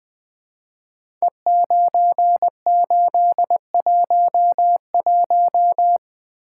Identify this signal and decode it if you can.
Morse code, heard as a keyed tone, about 20 wpm, 700 Hz. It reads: E9811
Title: Code Groups - 5 Letter:Numbers - 20wpm